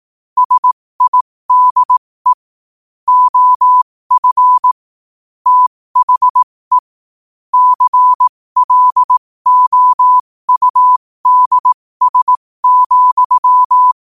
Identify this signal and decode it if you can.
Morse code, heard as a keyed tone, about 18 wpm, spaced as for 15 wpm, 1000 Hz. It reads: SIDE OF THE CLOUDS,
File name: qotdfiles / 15wpm